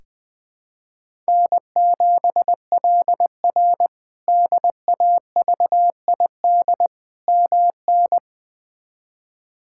Morse code, heard as a keyed tone, about 20 wpm, 700 Hz. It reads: N7LR DAVID MN
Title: K1USN SST QSO Practice - 20wpm